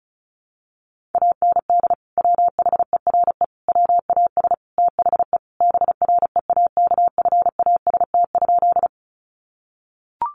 Morse code, heard as a keyed tone, about 34 wpm, 700 Hz. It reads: AND WHERE WAS THE BREAKFAST?
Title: Sentences from Top 1500 Words 35wpm